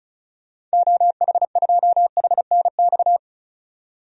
Morse code, heard as a keyed tone, about 36 wpm, 700 Hz. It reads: OH2HNX
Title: DXCC-100 Call Signs - 6 Characters - 35wpm